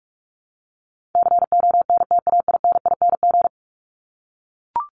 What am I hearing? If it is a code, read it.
Morse code, heard as a keyed tone, about 44 wpm, 700 Hz. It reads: CONTAINING
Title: ING Suffix - Top 100 Words with Base Words 45wpm